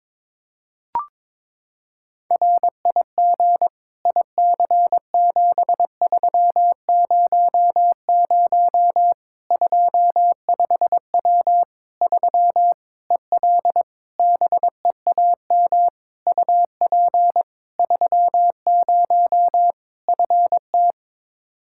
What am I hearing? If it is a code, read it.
Morse code, heard as a keyed tone, about 22 wpm, 700 Hz. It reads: RIG IC7300 25W 3 EL BEAM UP 30 FT
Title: Practice QSO - Exchange 2 - Info - 22wpm